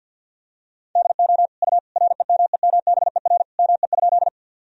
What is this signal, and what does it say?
Morse code, heard as a keyed tone, about 51 wpm, 700 Hz. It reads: DO U REMEMBER ME?
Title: Sentences from Top 400 Words and Abbreviations 50wpm